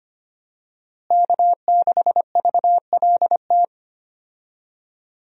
Morse code, heard as a keyed tone, about 25 wpm, 700 Hz. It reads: K6VLT
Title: USA Call Signs - 5 Characters - Repeated 3x - 25wpm